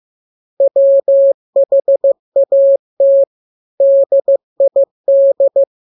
Morse code, heard as a keyed tone, about 15 wpm, 550 Hz. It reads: WHAT DID
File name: jokes / joke 2 15 0000